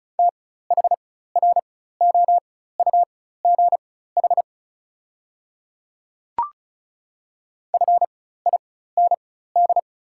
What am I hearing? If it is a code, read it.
Morse code, heard as a keyed tone, about 36 wpm, spaced as for 16 wpm, 700 Hz. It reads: THROUGH FIND
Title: 4x Character Spacing - Top 200 Words - 35wpm